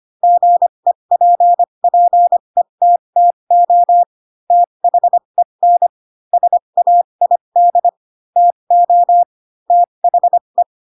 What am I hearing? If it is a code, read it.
Morse code, heard as a keyed tone, about 25 wpm, spaced as for 22 wpm, 700 Hz. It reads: GEPPETTO THEN SAID TO THE